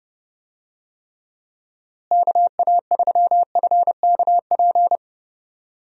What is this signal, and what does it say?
Morse code, heard as a keyed tone, about 32 wpm, 700 Hz. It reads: KA3FKP
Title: USA Call Signs - 6 Characters - Repeated 3x - 30wpm